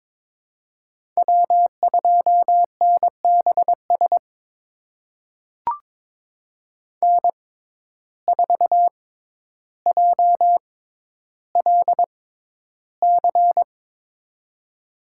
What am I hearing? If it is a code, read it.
Morse code, heard as a keyed tone, about 22 wpm, spaced as for 11 wpm, 700 Hz. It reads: W2NBS N4JLC
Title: USA Call Signs - 6x Character Spacing - 22wpm